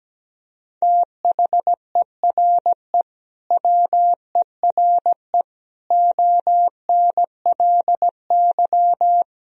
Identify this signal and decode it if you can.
Morse code, heard as a keyed tone, about 17 wpm, 700 Hz. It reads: THERE WERE ONLY
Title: Sentences from Top 1000 Words - Review Part 2 - 17wpm